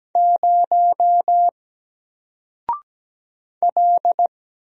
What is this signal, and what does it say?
Morse code, heard as a keyed tone, about 17 wpm, 700 Hz. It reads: 0 L
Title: Single Letter-Number 17wpm